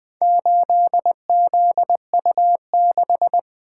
Morse code, heard as a keyed tone, about 20 wpm, 700 Hz. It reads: 8ZU6
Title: Four Letter-Number Combinations - 20wpm